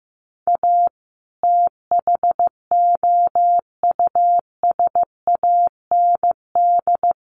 Morse code, heard as a keyed tone, about 15 wpm, 700 Hz. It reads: A THOUSAND
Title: Sentences from Top 600 Words 15wpm